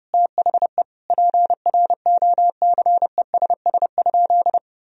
Morse code, heard as a keyed tone, about 30 wpm, 700 Hz. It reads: THE PROCESS?
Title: Sentences from Top 1000 Words - Review Part 3 - 30wpm